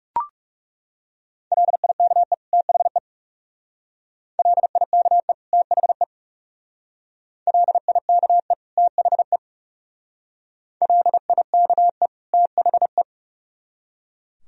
Morse code, heard as a keyed tone, about 38 wpm, 700 Hz. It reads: LIKE THE LIKE THE LIKE THE LIKE THE